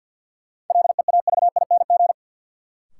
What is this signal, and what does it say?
Morse code, heard as a keyed tone, about 46 wpm, 700 Hz. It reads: LEAVING
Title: Sentences from Top 1000 Words - Review Part 4 - New Word Drill - 50wpm